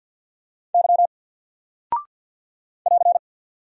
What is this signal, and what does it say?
Morse code, heard as a keyed tone, about 50 wpm, 700 Hz. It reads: Y <AR>
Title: Single Character (Letter:Number:Punctuation:Prosign) 50wpm